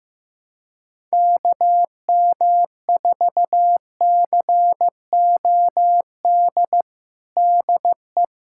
Morse code, heard as a keyed tone, about 16 wpm, 700 Hz. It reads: KM4COD DE
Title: USA Callsign DE Callsign - 6 Characters - 15wpm